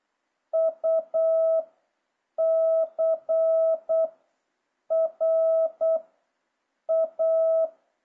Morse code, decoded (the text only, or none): U C R A